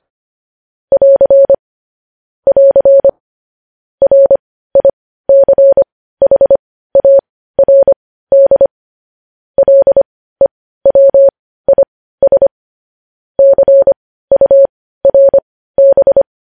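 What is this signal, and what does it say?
Morse code, heard as a keyed tone, about 25 wpm, spaced as for 15 wpm, 550 Hz. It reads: <AR> <AR> RICHARD LEWIS CURB
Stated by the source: Morse code news headlines at 15.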